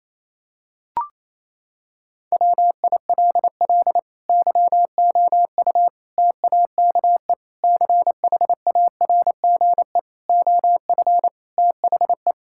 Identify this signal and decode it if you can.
Morse code, heard as a keyed tone, about 27 wpm, 700 Hz. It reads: WILL YOU TAKE CHARGE OF THE
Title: Sentences from Top 1000 Words - Review Part 6 - 28wpm